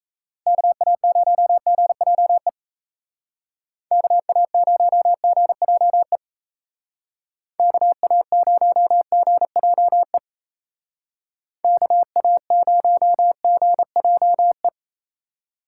Morse code, heard as a keyed tone, about 36 wpm, 700 Hz. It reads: KA0GJE KA0GJE KA0GJE KA0GJE